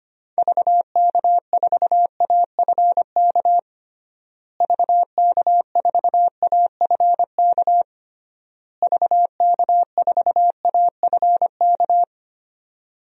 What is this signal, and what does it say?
Morse code, heard as a keyed tone, about 25 wpm, 700 Hz. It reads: VK4AFK VK4AFK VK4AFK
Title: DXCC-100 Call Signs - 6 Characters - Repeated 3x - 25wpm